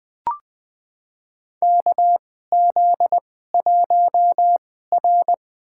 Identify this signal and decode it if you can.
Morse code, heard as a keyed tone, about 20 wpm, spaced as for 14 wpm, 700 Hz. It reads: KZ1R
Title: USA Call Signs - 4 Characters - 2x Character Spacing - 20wpm